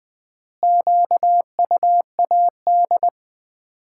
Morse code, heard as a keyed tone, about 20 wpm, 700 Hz. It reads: QUAD
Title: Common QSO Phrases - 3x Word-Spacing - 20wpm